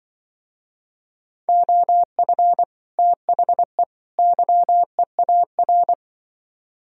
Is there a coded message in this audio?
Yes: Morse code from 1.5 to 5.9 s, heard as a keyed tone, about 26 wpm, 700 Hz. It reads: OF THE YEAR